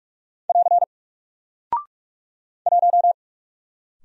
Morse code, heard as a keyed tone, about 44 wpm, 700 Hz. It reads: <AR> 1
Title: Single Character (Letter:Number:Punctuation:Prosign) 45wpm